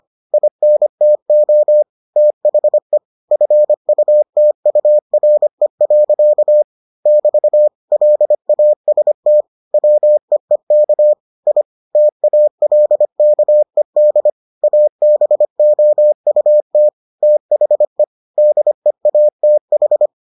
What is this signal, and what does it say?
Morse code, heard as a keyed tone, about 25 wpm, 600 Hz. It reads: INTO THE FUTURE. <BT> LAST WEEK I TALKED ABOUT THE DEATH